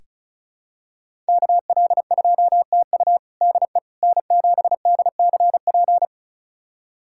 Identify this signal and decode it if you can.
Morse code, heard as a keyed tone, about 36 wpm, 700 Hz. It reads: KL2TU DE N7DCP
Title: USA Callsign DE Callsign - 5 Characters - 35wpm